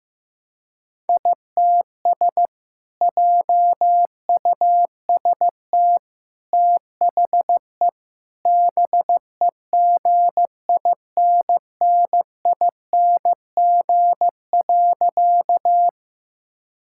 Morse code, heard as a keyed tone, about 15 wpm, 700 Hz. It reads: ITS JUST THE BEGINNING.
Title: Sentences from Top 1000 Words - Review Part 5 - 15wpm